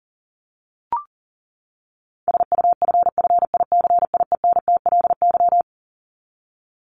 Morse code, heard as a keyed tone, about 40 wpm, 700 Hz. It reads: SUFFICIENTLY
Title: LY Suffix - Top 100 Words with Base Words 40wpm